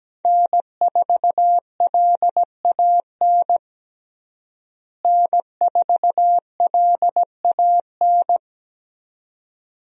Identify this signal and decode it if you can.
Morse code, heard as a keyed tone, about 17 wpm, 700 Hz. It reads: N4LAN N4LAN
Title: USA Call Signs - 5 Characters - Repeated 2x - 17wpm